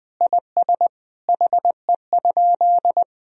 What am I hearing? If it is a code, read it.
Morse code, heard as a keyed tone, about 21 wpm, 700 Hz. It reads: IS HE?
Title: Sentences from Top 1000 Words - Review Part 4 - 20wpm